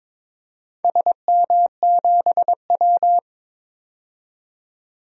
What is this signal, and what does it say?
Morse code, heard as a keyed tone, about 22 wpm, 700 Hz. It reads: SM7W
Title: DXCC-100 Call Signs - 4 Characters - 22wpm